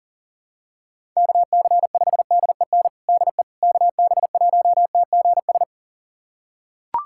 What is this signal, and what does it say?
Morse code, heard as a keyed tone, about 37 wpm, 700 Hz. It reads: KC5DEN DE KB1TGS
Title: USA Callsign DE Callsign - 6 Characters - 40wpm